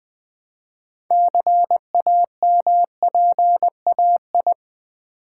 Morse code, heard as a keyed tone, about 20 wpm, 700 Hz. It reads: CAMPAI
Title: Build A Word - 2 Letters At A Time - 8-Letter Words - 20wpm